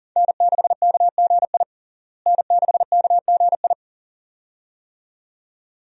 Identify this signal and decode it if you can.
Morse code, heard as a keyed tone, about 40 wpm, 700 Hz. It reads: N6KGI N6KGI
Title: USA Call Signs - 5 Characters - Repeated 2x - 40wpm